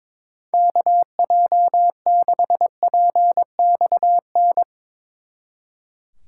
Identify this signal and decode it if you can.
Morse code, heard as a keyed tone, about 22 wpm, 700 Hz. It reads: KJ6PXN
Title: Speed Racing - USA 2x3 Call Signs 22wpm